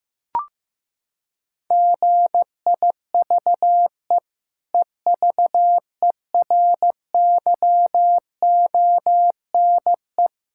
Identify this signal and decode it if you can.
Morse code, heard as a keyed tone, about 15 wpm, 700 Hz. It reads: GIVE EVERYONE
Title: Sentences from Top 1000 Words - Review Part 6 - 15wpm